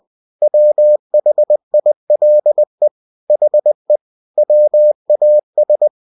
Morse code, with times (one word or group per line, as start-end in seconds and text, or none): 0.4-2.9 WHILE
3.3-4.0 HE
4.4-5.9 WAS